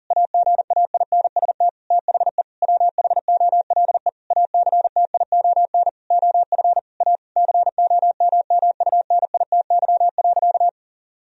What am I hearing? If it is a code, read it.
Morse code, heard as a keyed tone, about 40 wpm, 700 Hz. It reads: AGAINST THE WHOLE ACTION OF A COMMUNITY.
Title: Sentences from Top 1000 Words - Review Part 4 - 40wpm